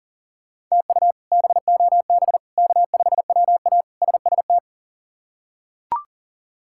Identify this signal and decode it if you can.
Morse code, heard as a keyed tone, about 39 wpm, 690 Hz. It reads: TU BOB K5WA SST
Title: K1USN SST QSO Practice - 40wpm